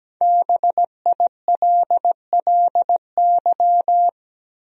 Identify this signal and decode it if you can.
Morse code, heard as a keyed tone, about 17 wpm, 700 Hz. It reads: BILLY
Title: USA Common Names - 17wpm